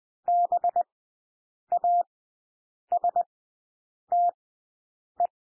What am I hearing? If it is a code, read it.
Morse code, heard as a keyed tone, about 20 wpm, spaced as for 8 wpm, 700 Hz. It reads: BASTE
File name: cwmp3 / W1 SOUND WORDS 20 wpm